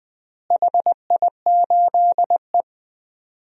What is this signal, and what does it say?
Morse code, heard as a keyed tone, about 20 wpm, 700 Hz. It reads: HI8E
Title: DXCC-100 Call Signs - 4 Characters - Repeated 3x - 20wpm